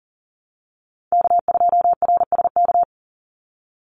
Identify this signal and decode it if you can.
Morse code, heard as a keyed tone, about 41 wpm, 700 Hz. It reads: K2RSK
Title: Call Signs - 40wpm